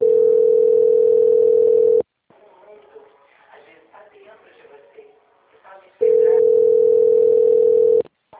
Ringback tone (North America), 440 + 480 Hz.